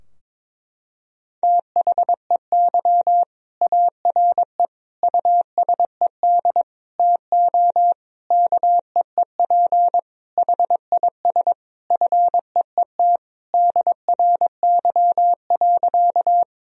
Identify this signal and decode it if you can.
Morse code, heard as a keyed tone, about 22 wpm, 700 Hz. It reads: THEY ARE USED TO KEEP HIS FEET DRY.
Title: Sentences from Top 1000 Words - Review Part 3 - 22wpm